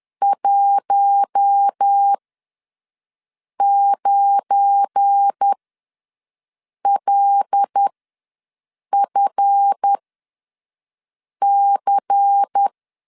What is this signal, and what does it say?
Morse code, heard as a keyed tone, about 10 wpm, spaced as for 5 wpm, 800 Hz. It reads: 19LFC
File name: LEZIONI / 7 GF9Z / 7 GF9 ws 10wpm 4034